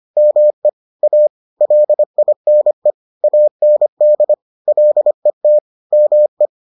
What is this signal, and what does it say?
Morse code, heard as a keyed tone, about 25 wpm, 600 Hz. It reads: ME A LINE AND LET ME